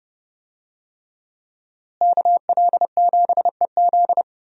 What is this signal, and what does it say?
Morse code, heard as a keyed tone, about 31 wpm, 700 Hz. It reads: KL7EZ
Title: DXCC-100 Call Signs - 5 Characters - Repeated 3x - 30wpm